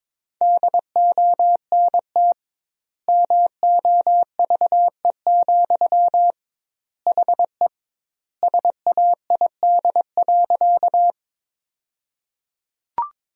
Morse code, heard as a keyed tone, about 22 wpm, 700 Hz. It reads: DONT MOVE, HE SAID.
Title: 2x Word-Spacing - Sentences from Top 300 Words - 22wpm